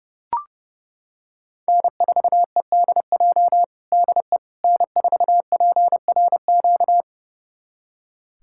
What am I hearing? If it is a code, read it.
Morse code, heard as a keyed tone, about 30 wpm, 700 Hz. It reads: N4EDJ DE N4PRQ
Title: USA Callsign DE Callsign - 5 Characters - 30wpm